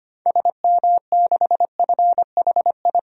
Morse code, heard as a keyed tone, about 25 wpm, 700 Hz. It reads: SM6FHI
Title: DXCC-100 Call Signs - 6 Characters - 25wpm